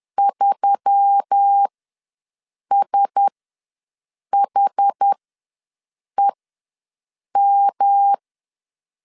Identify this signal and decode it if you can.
Morse code, heard as a keyed tone, about 10 wpm, spaced as for 6 wpm, 800 Hz. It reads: 3SHEM
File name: LEZIONI / 5 LBVC / 5 LB ws 10wpm 3616